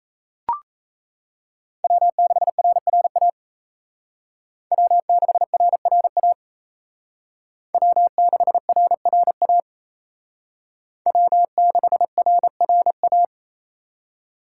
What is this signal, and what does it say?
Morse code, heard as a keyed tone, about 35 wpm, 700 Hz. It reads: W6RRA W6RRA W6RRA W6RRA
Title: Speed Racing - USA 1x3 Call Signs 28wpm